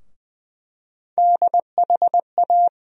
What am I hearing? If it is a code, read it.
Morse code, heard as a keyed tone, about 20 wpm, 700 Hz. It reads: DHA
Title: USA Call Sign Postfixes - 3 Letters 20wpm